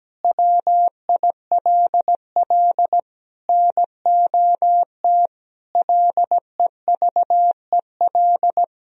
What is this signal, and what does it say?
Morse code, heard as a keyed tone, about 17 wpm, 700 Hz. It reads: WILL NOT LEVEL
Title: Sentences from Top 1000 Words - Review Part 4 - 17wpm